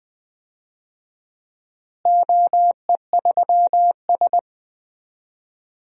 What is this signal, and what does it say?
Morse code, heard as a keyed tone, about 20 wpm, 700 Hz. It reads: OE3S
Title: DXCC-100 Call Signs - 4 Characters - Repeated 3x - 20wpm